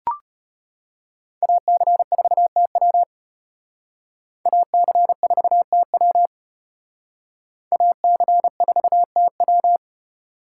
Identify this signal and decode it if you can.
Morse code, heard as a keyed tone, about 32 wpm, 700 Hz. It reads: AC4TW AC4TW AC4TW